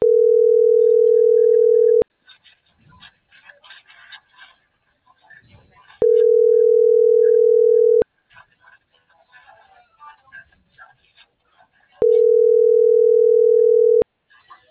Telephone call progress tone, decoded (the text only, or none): ringback tone